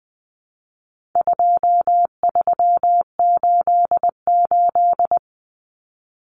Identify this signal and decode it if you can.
Morse code, heard as a keyed tone, about 20 wpm, 700 Hz. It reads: 2388
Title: Numbers - 4-Digit 20wpm